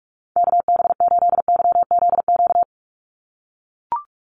Morse code, heard as a keyed tone, about 44 wpm, 700 Hz. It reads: KB8YZQ
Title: USA Call Signs 45wpm